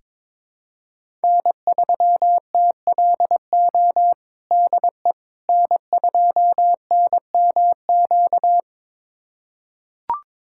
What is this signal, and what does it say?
Morse code, heard as a keyed tone, about 21 wpm, 700 Hz. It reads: N3TLO DE N2NMQ
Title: USA Callsign DE Callsign - 5 Characters - 22wpm